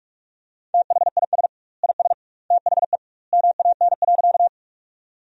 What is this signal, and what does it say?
Morse code, heard as a keyed tone, about 46 wpm, 700 Hz. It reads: THIS IS THE MAN.
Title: 2x Word-Spacing - Sentences from Top 200 Words - 45wpm